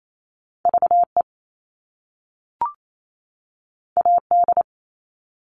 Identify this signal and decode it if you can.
Morse code, heard as a keyed tone, about 28 wpm, 700 Hz. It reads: VE AD
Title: All Cut Numbers - 2-Digit 28wpm